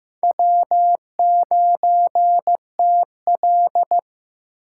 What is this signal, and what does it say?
Morse code, heard as a keyed tone, about 15 wpm, 700 Hz. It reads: W9TL
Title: USA Call Signs - 4 Characters - 8x Character Spacing - 15wpm